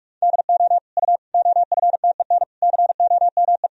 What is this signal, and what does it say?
Morse code, heard as a keyed tone, about 46 wpm, 700 Hz. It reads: DO U OFTEN COME
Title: Sentences from Top 300 Words and Abbreviations 45wpm